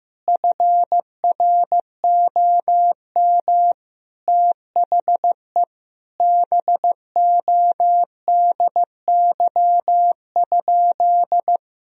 Morse code, heard as a keyed tone, about 15 wpm, 700 Hz. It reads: FROM THE BODY?
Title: Sentences from Top 1000 Words - Review Part 2 - 15wpm